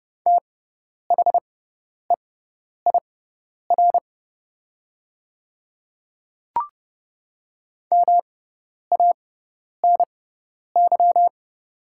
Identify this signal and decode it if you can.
Morse code, heard as a keyed tone, about 30 wpm, spaced as for 10 wpm, 700 Hz. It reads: THEIR MANY
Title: 6x Character Spacing - Top 100 Words - 30wpm